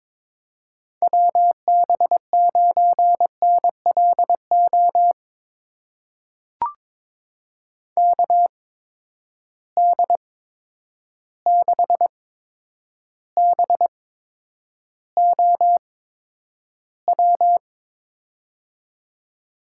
Morse code, heard as a keyed tone, about 22 wpm, spaced as for 9 wpm, 700 Hz. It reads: WB9NLO KD6BOW
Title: USA Call Signs - 8x Character Spacing - 22wpm